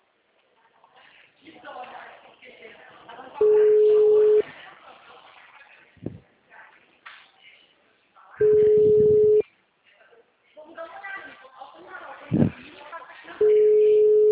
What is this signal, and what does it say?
Ringback tone (Europe, 425 Hz), with speech.